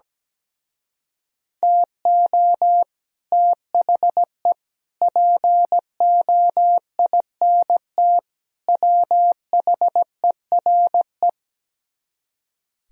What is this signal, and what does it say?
Morse code, heard as a keyed tone, about 17 wpm, 700 Hz. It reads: TO THE POINT WHERE
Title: Speed-Racing - Sets of 4 Words 17wpm